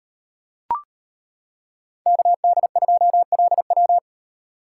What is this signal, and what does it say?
Morse code, heard as a keyed tone, about 33 wpm, 700 Hz. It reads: KD2LW